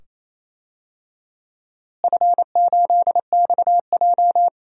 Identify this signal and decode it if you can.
Morse code, heard as a keyed tone, about 28 wpm, 700 Hz. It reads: F8XJ
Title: DXCC-100 Call Signs - 4 Characters - Repeated 2x - 28wpm